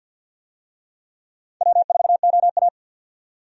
Morse code, heard as a keyed tone, about 50 wpm, 700 Hz. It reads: W4OA
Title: USA Call Signs - 4 Characters - 50wpm